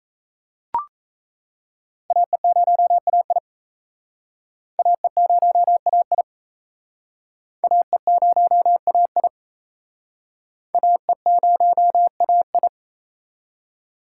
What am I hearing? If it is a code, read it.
Morse code, heard as a keyed tone, about 37 wpm, 700 Hz. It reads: AE0AI AE0AI AE0AI AE0AI